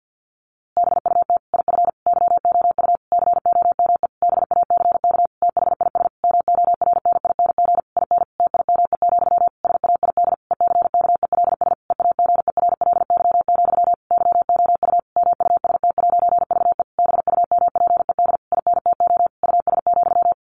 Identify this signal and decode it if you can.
Morse code, heard as a keyed tone, about 50 wpm, 700 Hz. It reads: BUT IF YOU COME BACK THIS MORNING IN TIME, SAID ECKELS EAGERLY, YOU MUST'VE BUMPED INTO US,
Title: A Sound Of Thunder 50wpm